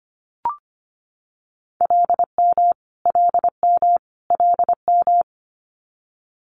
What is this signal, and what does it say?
Morse code, heard as a keyed tone, about 25 wpm, 700 Hz. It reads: LM LM LM
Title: Multi-letter Phonemes Repeated 3x 25wpm